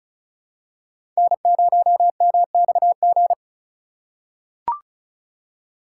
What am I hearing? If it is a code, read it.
Morse code, heard as a keyed tone, about 34 wpm, 700 Hz. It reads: N0MXG
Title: USA Call Signs - 5 Characters - 4x Character Spacing - 35wpm